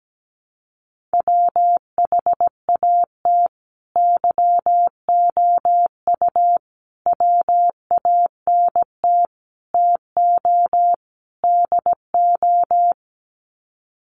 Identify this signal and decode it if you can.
Morse code, heard as a keyed tone, about 17 wpm, 700 Hz. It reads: WHAT YOU WANT TO DO
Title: Sets of 5 Words Encore 17wpm